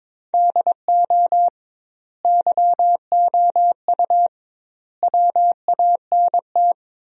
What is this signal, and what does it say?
Morse code, heard as a keyed tone, about 22 wpm, 700 Hz. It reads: DO YOU WANT
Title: 2x Word-Spacing - Sentences from Top 300 Words - 22wpm